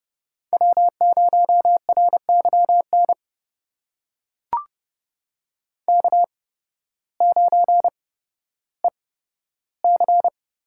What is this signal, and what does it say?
Morse code, heard as a keyed tone, about 30 wpm, spaced as for 14 wpm, 700 Hz. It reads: W0RYN K9EC
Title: USA Call Signs - 8x Character Spacing - 30wpm